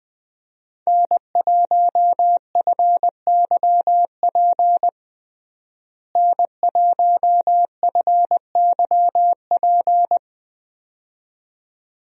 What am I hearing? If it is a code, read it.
Morse code, heard as a keyed tone, about 20 wpm, 700 Hz. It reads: N1FYP N1FYP
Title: USA Call Signs - 5 Characters - Repeated 2x - 20wpm